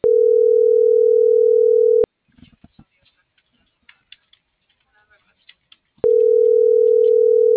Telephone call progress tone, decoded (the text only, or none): ringback tone